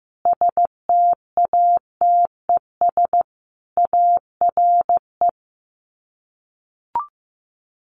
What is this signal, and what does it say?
Morse code, heard as a keyed tone, about 15 wpm, 700 Hz. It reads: STATES ARE
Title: Sets of 5 Words Encore 15wpm